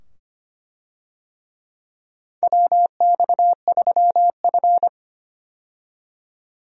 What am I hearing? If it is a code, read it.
Morse code, heard as a keyed tone, about 25 wpm, 700 Hz. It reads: WX3F
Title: USA Call Signs - 4 Characters - 25wpm